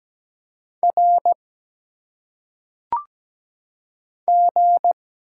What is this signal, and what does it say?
Morse code, heard as a keyed tone, about 17 wpm, 700 Hz. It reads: R G
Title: Single Letters - Phonetic Pronunciation 17wpm